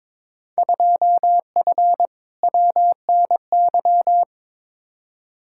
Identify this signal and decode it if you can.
Morse code, heard as a keyed tone, about 22 wpm, 700 Hz. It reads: 2F WNY
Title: Field Day Exchanges Encore - 22wpm